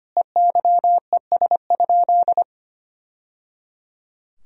Morse code, heard as a keyed tone, about 25 wpm, 700 Hz. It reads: EYES?
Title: Sentences from Top 1000 Words - Review Part 4 - 25wpm